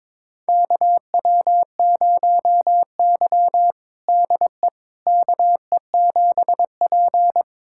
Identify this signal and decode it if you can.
Morse code, heard as a keyed tone, about 22 wpm, 700 Hz. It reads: KW0Y DE KE7P
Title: USA Callsign DE Callsign - 4 Characters - 22wpm